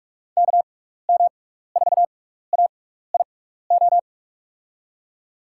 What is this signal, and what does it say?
Morse code, heard as a keyed tone, about 44 wpm, spaced as for 15 wpm, 700 Hz. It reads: KM4AIO
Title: USA Call Signs - 6 Characters - 6x Character Spacing - 45wpm